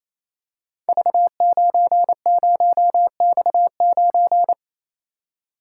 Morse code, heard as a keyed tone, about 28 wpm, 700 Hz. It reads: V90X9
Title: Code Groups - 5 Letter:Numbers - 28wpm